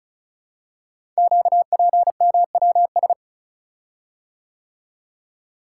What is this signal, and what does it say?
Morse code, heard as a keyed tone, about 36 wpm, 690 Hz. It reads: QPMWS
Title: Code Groups - 5 Letters - 35wpm